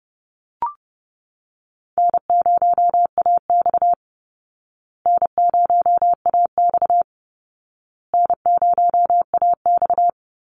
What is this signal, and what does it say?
Morse code, heard as a keyed tone, about 30 wpm, 700 Hz. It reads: N0AX N0AX N0AX
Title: Easy Call Signs Repeated 3x - 30wpm